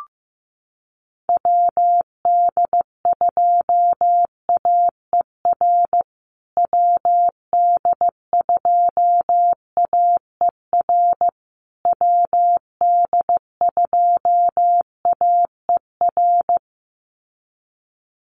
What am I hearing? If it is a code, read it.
Morse code, heard as a keyed tone, about 15 wpm, 700 Hz. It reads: WD2AER WD2AER WD2AER
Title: USA Call Signs Repeated 3x 15wpm